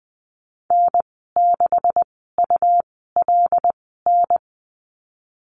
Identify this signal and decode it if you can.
Morse code, heard as a keyed tone, about 20 wpm, spaced as for 14 wpm, 700 Hz. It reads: N6ULN
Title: 128 Lesson 21g - Number 6 - Call Signs 20wpm